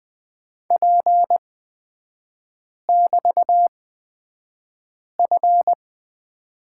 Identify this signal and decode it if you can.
Morse code, heard as a keyed tone, about 20 wpm, spaced as for 5 wpm, 700 Hz. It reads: P<BT>F
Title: 224 Lesson 36e - BT - Three Characters 20wpm